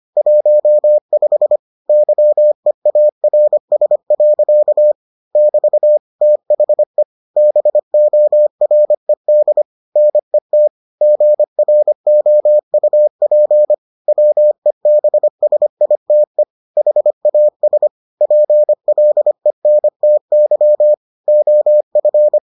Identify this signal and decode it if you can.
Morse code, heard as a keyed tone, about 25 wpm, 600 Hz. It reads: 15 YEARS. <BT> THE BORED NET GROUP WEBSITE HAS PLENTY OF